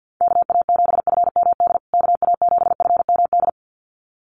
Morse code, heard as a keyed tone, about 50 wpm, 700 Hz. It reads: KA7FMD KA7FMD
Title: USA Call Signs Repeated 2x 50wpm